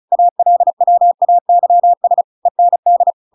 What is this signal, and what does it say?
Morse code, heard as a keyed tone, about 34 wpm, 700 Hz. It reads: ALWAYS END